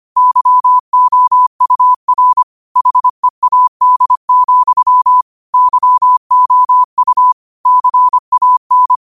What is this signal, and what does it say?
Morse code, heard as a keyed tone, about 25 wpm, 1000 Hz. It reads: YOUR HEAD, YOU CAN
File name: qotdfiles / 25wpm